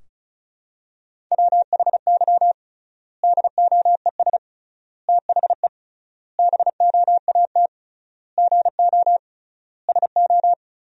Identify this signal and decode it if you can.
Morse code, heard as a keyed tone, about 36 wpm, 700 Hz. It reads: WHY DOES THE BOAT GO SO
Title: 3x Word-Spacing - Sentences from Top 500 Words - 35wpm